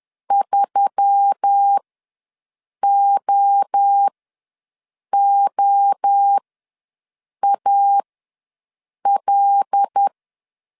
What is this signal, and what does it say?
Morse code, heard as a keyed tone, about 10 wpm, spaced as for 6 wpm, 800 Hz. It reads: 3OOAL
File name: LEZIONI / 5 LBVC / 5 L ws 10wpm 3636